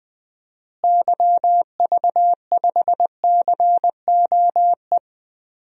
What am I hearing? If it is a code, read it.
Morse code, heard as a keyed tone, about 19 wpm, 700 Hz. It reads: YV5COE
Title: DXCC-100 Call Signs - 6 Characters - Repeated 3x - 20wpm